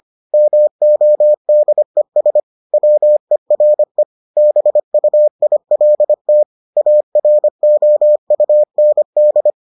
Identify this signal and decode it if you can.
Morse code, heard as a keyed tone, about 25 wpm, 600 Hz. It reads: MODES WERE BUILT AROUND